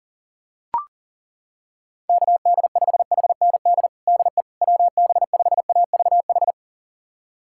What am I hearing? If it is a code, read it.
Morse code, heard as a keyed tone, about 40 wpm, 690 Hz. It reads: KD5HND DE WB5AVH
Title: USA Callsign DE Callsign - 6 Characters - 40wpm